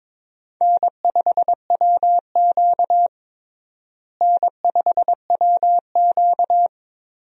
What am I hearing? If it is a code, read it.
Morse code, heard as a keyed tone, about 21 wpm, 700 Hz. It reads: N5WQ N5WQ
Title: USA Call Signs - 4 Characters - Repeated 2x - 22wpm